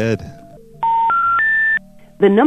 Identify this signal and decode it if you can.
Special information tone, with speech.